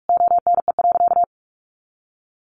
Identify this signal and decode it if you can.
Morse code, heard as a keyed tone, about 44 wpm, 700 Hz. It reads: ONE.
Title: Additional Easy Phrases 45wpm